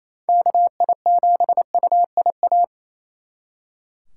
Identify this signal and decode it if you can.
Morse code, heard as a keyed tone, about 37 wpm, 700 Hz. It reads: KI7UIA